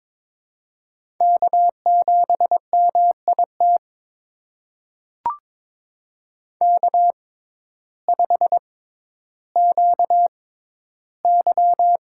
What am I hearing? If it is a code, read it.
Morse code, heard as a keyed tone, about 23 wpm, spaced as for 12 wpm, 700 Hz. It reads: K7MIT K5QY
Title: USA Call Signs - 6x Character Spacing - 22wpm